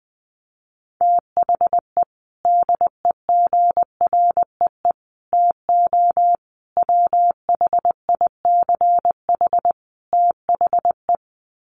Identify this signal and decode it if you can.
Morse code, heard as a keyed tone, about 19 wpm, 700 Hz. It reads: THE DEGREE TO WHICH THE
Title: Sets of 5 Words Encore2 20wpm